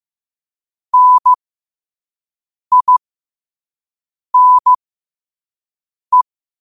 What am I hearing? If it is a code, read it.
Morse code, heard as a keyed tone, about 15 wpm, spaced as for 5 wpm, 1000 Hz. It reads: NINE
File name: qotdfiles / 5wpm